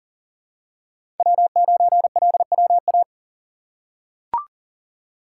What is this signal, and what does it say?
Morse code, heard as a keyed tone, about 42 wpm, 700 Hz. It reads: W9LWA
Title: USA Call Signs - 5 Characters - 6x Character Spacing - 40wpm